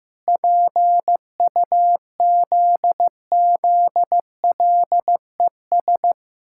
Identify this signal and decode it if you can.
Morse code, heard as a keyed tone, about 15 wpm, 700 Hz. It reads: PUZZLES
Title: Pangram Challenge v2 - 15wpm